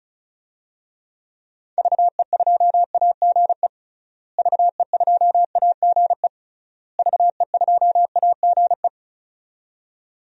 Morse code, heard as a keyed tone, about 36 wpm, 700 Hz. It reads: VE2AGE VE2AGE VE2AGE
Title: DXCC-100 Call Signs - 6 Characters - Repeated 3x - 35wpm